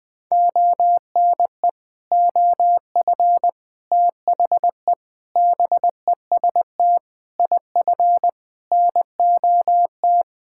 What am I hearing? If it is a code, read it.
Morse code, heard as a keyed tone, about 21 wpm, 700 Hz. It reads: ONE OF THE BEST IF NOT
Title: Sentences with 8-Word Phrases - 20wpm